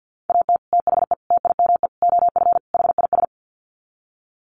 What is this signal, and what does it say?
Morse code, heard as a keyed tone, about 50 wpm, 700 Hz. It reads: AT THE TIME OF HIS
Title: Sets of 5 Words Encore2 50wpm